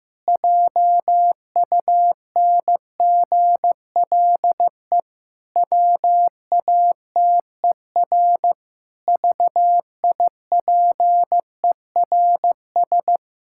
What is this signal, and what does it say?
Morse code, heard as a keyed tone, about 15 wpm, 700 Hz. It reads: JUNGLE WATER VIPERS
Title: Pangram Challenge v2 - 15wpm